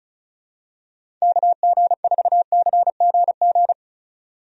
Morse code, heard as a keyed tone, about 34 wpm, 700 Hz. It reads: KG4CGG
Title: USA Call Signs - 6 Characters - 35wpm